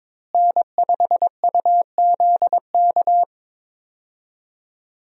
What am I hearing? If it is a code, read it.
Morse code, heard as a keyed tone, about 22 wpm, 700 Hz. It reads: N5UZK
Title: USA Call Signs - 5 Characters - Repeated 2x - 22wpm